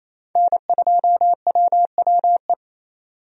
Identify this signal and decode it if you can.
Morse code, heard as a keyed tone, about 28 wpm, 700 Hz. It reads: N2WWE
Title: USA Call Signs - 5 Characters - 28wpm